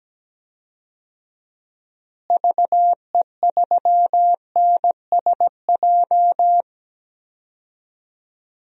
Morse code, heard as a keyed tone, about 17 wpm, 700 Hz. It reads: VE3NSJ
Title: DXCC-100 Call Signs - 6 Characters - Repeated 3x - 17wpm